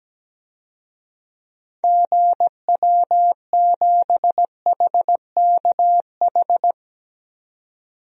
Morse code, heard as a keyed tone, about 17 wpm, 700 Hz. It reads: GW7HKH
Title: DXCC-100 Call Signs - 6 Characters - 17wpm